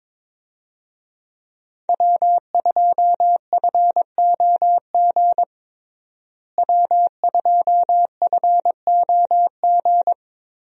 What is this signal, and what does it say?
Morse code, heard as a keyed tone, about 22 wpm, 700 Hz. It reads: W2FOG W2FOG
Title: USA Call Signs - 5 Characters - Repeated 3x - 22wpm